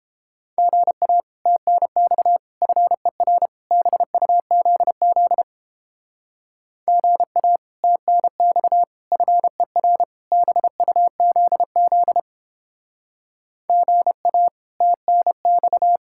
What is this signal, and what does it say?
Morse code, heard as a keyed tone, about 29 wpm, 700 Hz. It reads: GA TNX FER BUZZ GA TNX FER BUZZ GA TNX